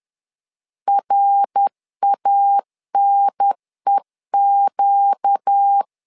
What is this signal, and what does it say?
Morse code, heard as a keyed tone, about 10 wpm, 800 Hz. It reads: RANEQ
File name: LEZIONI / 4 Q1 10wpm 2448